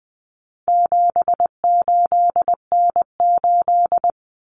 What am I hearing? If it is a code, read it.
Morse code, heard as a keyed tone, about 20 wpm, 700 Hz. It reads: 78N8
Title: Common Cut Numbers - 4-Digit 20wpm